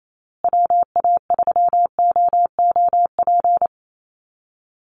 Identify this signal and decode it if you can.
Morse code, heard as a keyed tone, about 28 wpm, 700 Hz. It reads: WA3OOP
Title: USA Call Signs Repeated 3x 28wpm